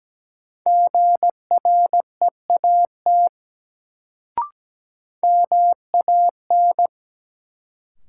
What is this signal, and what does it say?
Morse code, heard as a keyed tone, about 17 wpm, 700 Hz. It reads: GREAT MAN
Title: Sets of 2 Words - New Word Drill - 17wpm